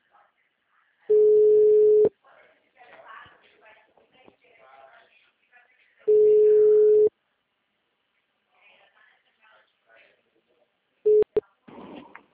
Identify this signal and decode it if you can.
Ringback tone (Europe, 425 Hz), with speech.
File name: out- -2008- - - .37497